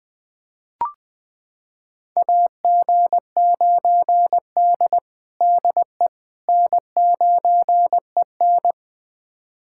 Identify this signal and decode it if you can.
Morse code, heard as a keyed tone, about 20 wpm, 700 Hz. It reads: AG9D DE N9EN
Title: USA Callsign DE Callsign - 4 Characters - 20wpm